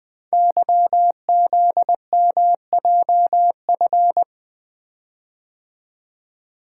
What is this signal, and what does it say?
Morse code, heard as a keyed tone, about 20 wpm, 700 Hz. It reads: YZMJF
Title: Code Groups - 5 Letters - 20wpm